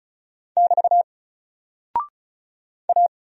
Morse code, heard as a keyed tone, about 36 wpm, 700 Hz. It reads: <BT> A
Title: Single Character (Letter:Number:Punctuation:Prosign) 35wpm